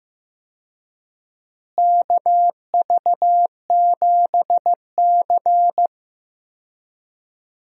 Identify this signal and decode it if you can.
Morse code, heard as a keyed tone, about 15 wpm, 700 Hz. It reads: KV7C
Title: USA Call Signs - 4 Characters - Repeated 3x - 15wpm